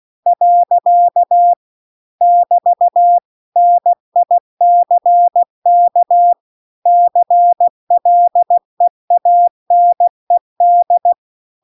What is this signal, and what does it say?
Morse code, heard as a keyed tone, about 16 wpm, 700 Hz. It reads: . <BT> NICK CLEANED
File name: CW-The-Big-Twoheaded-River-2-16WPM